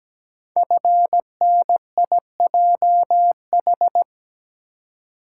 Morse code, heard as a keyed tone, about 17 wpm, 700 Hz. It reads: FNIJH
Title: Code Groups - 5 Letters - 17wpm